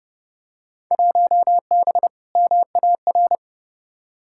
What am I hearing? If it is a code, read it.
Morse code, heard as a keyed tone, about 30 wpm, 700 Hz. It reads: 1B MAR
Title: Field Day Exchanges Encore - 30wpm